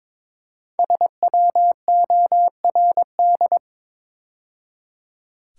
Morse code, heard as a keyed tone, about 22 wpm, 700 Hz. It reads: SWORD
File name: 5 Letter Words - First 3 Letters Given - 22wpm